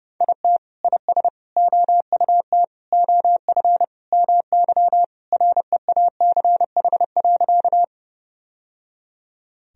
Morse code, heard as a keyed tone, about 32 wpm, 700 Hz. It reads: IT IS OUT OF MY REACH.
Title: Sentences from Top 1000 Words - Review Part 2 - 30wpm